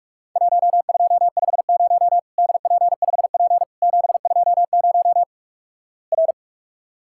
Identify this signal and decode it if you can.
Morse code, heard as a keyed tone, about 46 wpm, 700 Hz. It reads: 1250 DP5P 720 R
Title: Worked All Europe DX Contest - QTC Exchange - 45wpm